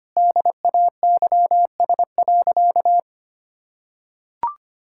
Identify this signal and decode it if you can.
Morse code, heard as a keyed tone, about 25 wpm, 700 Hz. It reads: DAYS.
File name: practice / 25wpm / Sentences with 4-Word Phrases - 25wpm